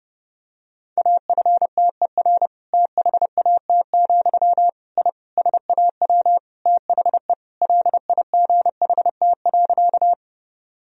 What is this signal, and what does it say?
Morse code, heard as a keyed tone, about 30 wpm, 700 Hz. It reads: AFTER THAT, I SAW THE LIGHT.
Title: Sentences from Top 1000 Words - Review Part 1 - 30wpm